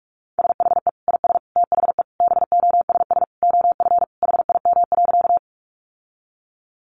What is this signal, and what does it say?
Morse code, heard as a keyed tone, about 44 wpm, 700 Hz. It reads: SHE IS THE BOSS OF HIM.
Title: Sentences from Top 700 Words 45wpm